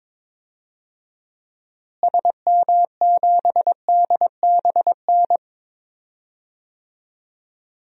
Morse code, heard as a keyed tone, about 22 wpm, 700 Hz. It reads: SM7DBN
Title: DXCC-100 Call Signs - 6 Characters - Repeated 3x - 22wpm